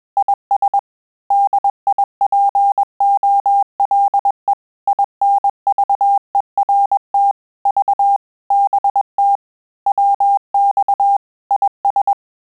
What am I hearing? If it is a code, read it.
Morse code, heard as a keyed tone, about 21 wpm, 800 Hz. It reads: IS DIPOLE INVERT V BT WX IS
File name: QSO CW 100 CPM